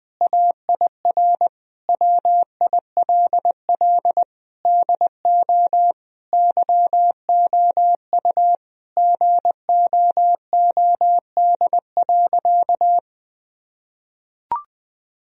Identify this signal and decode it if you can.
Morse code, heard as a keyed tone, about 20 wpm, 700 Hz. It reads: AIR WILL DO YOU GOOD.
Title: Sentences from Top 1000 Words - Review Part 1 - 20wpm